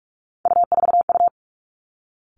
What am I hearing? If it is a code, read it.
Morse code, heard as a keyed tone, about 46 wpm, 700 Hz. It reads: U4U
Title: Three Challenging Letter-Number Combinations 45wpm